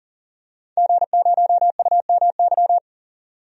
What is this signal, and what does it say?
Morse code, heard as a keyed tone, about 40 wpm, 700 Hz. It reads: G0UMY
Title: DXCC-100 Call Signs - 5 Characters - 40wpm